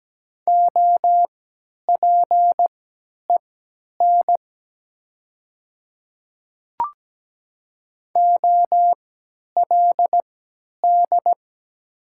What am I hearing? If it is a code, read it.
Morse code, heard as a keyed tone, about 17 wpm, spaced as for 10 wpm, 690 Hz. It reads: OPEN OLD
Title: 3x Character Spacing - Top 500 Words - 17wpm